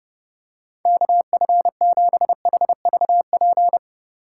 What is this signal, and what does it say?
Morse code, heard as a keyed tone, about 29 wpm, 700 Hz. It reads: KF7HVP